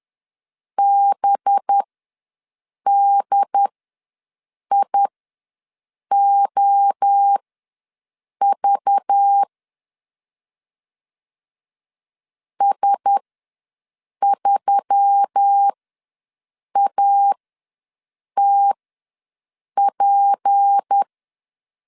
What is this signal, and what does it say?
Morse code, heard as a keyed tone, about 10 wpm, spaced as for 6 wpm, 800 Hz. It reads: BDIOV S3ATP
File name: LEZIONI / 6 PKX0 / 6 P ws 10wpm 3751